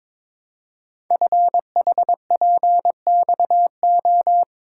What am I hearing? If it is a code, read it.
Morse code, heard as a keyed tone, about 22 wpm, 700 Hz. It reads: FHPXO
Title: Code Groups - 5 Letters - 22wpm